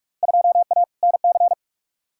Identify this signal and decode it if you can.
Morse code, heard as a keyed tone, about 42 wpm, 700 Hz. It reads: 2A NC
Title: Field Day Exchanges - 45wpm